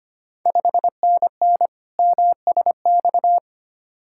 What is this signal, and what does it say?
Morse code, heard as a keyed tone, about 25 wpm, 700 Hz. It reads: 5NN MSX
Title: Ontario QSO Party Exchange 25wpm